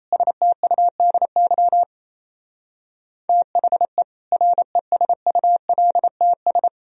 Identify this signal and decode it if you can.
Morse code, heard as a keyed tone, about 37 wpm, 700 Hz. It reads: STUDY THE RESULTS